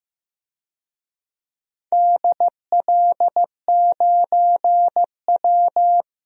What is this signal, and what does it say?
Morse code, heard as a keyed tone, about 16 wpm, 700 Hz. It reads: DL9W
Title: DXCC-100 Call Signs - 4 Characters - 15wpm